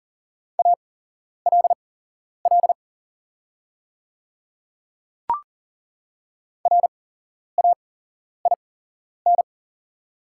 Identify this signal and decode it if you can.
Morse code, heard as a keyed tone, about 40 wpm, spaced as for 11 wpm, 690 Hz. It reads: ALL RAIN
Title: 8x Character Spacing - Top 300 Words - 40wpm